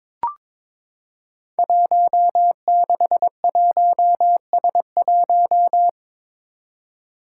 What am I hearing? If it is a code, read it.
Morse code, heard as a keyed tone, about 23 wpm, 700 Hz. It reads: 161S1
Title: Five Challenging Letter-Number Combinations 22wpm